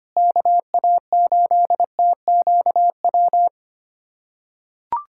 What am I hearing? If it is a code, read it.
Morse code, heard as a keyed tone, about 25 wpm, 700 Hz. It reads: KA8TQW
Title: USA Callsign DE Callsign 25wpm